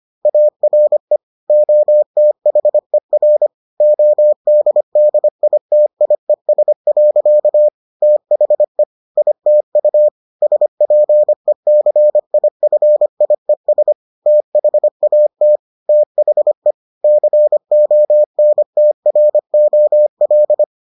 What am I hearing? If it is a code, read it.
Morse code, heard as a keyed tone, about 25 wpm, 600 Hz. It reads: ARE OTHER ODDITIES. THE ITU SPECIFIES THAT THE CONTROL